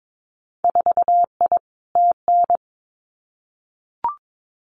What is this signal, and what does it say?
Morse code, heard as a keyed tone, about 22 wpm, 700 Hz. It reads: 4I TN
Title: Winter Field Day Exchanges 22wpm